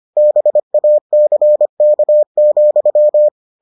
Morse code, heard as a keyed tone, about 25 wpm, 600 Hz. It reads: BACK,